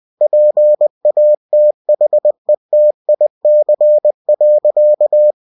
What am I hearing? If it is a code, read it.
Morse code, heard as a keyed tone, about 20 wpm, 600 Hz. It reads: PATHETIC.
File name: cw-books / winnie-the-pooh / winnie-20wpm-page-0044